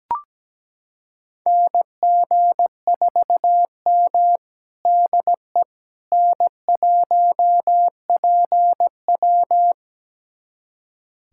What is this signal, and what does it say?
Morse code, heard as a keyed tone, about 17 wpm, 700 Hz. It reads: NG4M DE N1PW
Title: USA Callsign DE Callsign - 4 Characters - 17wpm